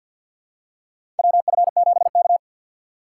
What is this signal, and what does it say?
Morse code, heard as a keyed tone, about 51 wpm, 700 Hz. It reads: WF7K
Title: USA Call Signs - 4 Characters - 2x Character Spacing - 50wpm